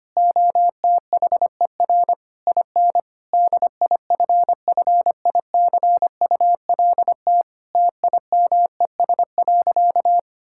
Morse code, heard as a keyed tone, about 25 wpm, 700 Hz. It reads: OTHER IN DIFFICULT TIMES.
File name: practice / 25wpm / Difficult Sentences with Hints - 25wpm